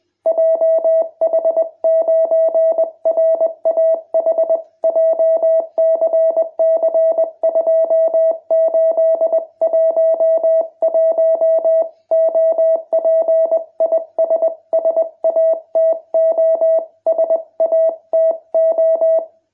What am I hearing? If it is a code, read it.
Morse code, heard as a keyed tone, about 21 wpm, 650 Hz. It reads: JH9RAH JCC2811 OP IS SATO SATO